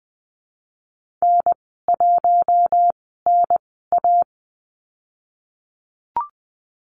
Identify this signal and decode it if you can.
Morse code, heard as a keyed tone, about 20 wpm, spaced as for 15 wpm, 700 Hz. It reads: N1NA
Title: 140 Lesson 23g - Letter f - Call Signs 20wpm